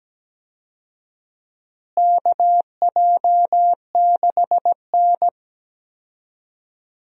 Morse code, heard as a keyed tone, about 17 wpm, 700 Hz. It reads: KJ6N
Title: USA Call Signs - 4 Characters - Repeated 3x - 17wpm